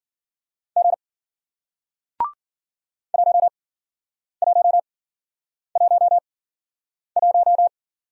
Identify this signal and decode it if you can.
Morse code, heard as a keyed tone, about 53 wpm, 700 Hz. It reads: G 1 1 1 1